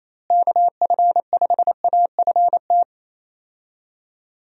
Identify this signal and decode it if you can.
Morse code, heard as a keyed tone, about 27 wpm, 700 Hz. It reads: KF5AFT
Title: USA Call Signs - 6 Characters - Repeated 3x - 28wpm